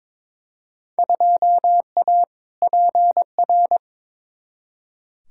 Morse code, heard as a keyed tone, about 22 wpm, 700 Hz. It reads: 2A PR
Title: Field Day Exchanges Encore - 22wpm